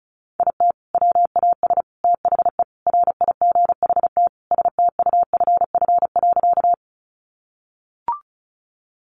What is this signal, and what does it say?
Morse code, heard as a keyed tone, about 34 wpm, 700 Hz. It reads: IT WAS THE RIGHT STUFF.
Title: Sentences from Top 1400 Words 35wpm